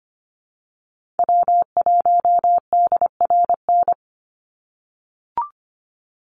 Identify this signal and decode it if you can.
Morse code, heard as a keyed tone, about 25 wpm, 700 Hz. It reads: W1DRN
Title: USA Call Signs 25wpm